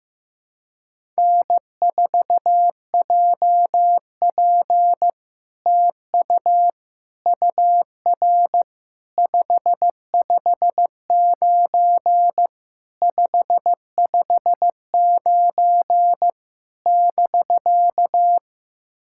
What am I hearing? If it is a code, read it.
Morse code, heard as a keyed tone, about 15 wpm, 700 Hz. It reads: N4JP TU UR 559 559 <BK>
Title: SOTA QSO Practice 15wpm